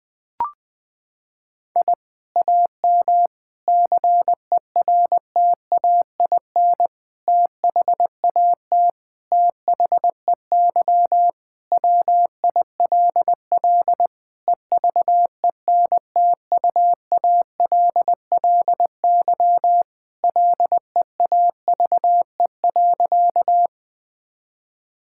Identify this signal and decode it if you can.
Morse code, heard as a keyed tone, about 20 wpm, 700 Hz. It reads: I AM CERTAIN THAT THEY WILL EVENTUALLY LEAVE.
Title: Sentences from Top 1000 Words - Review Part 6 - 20wpm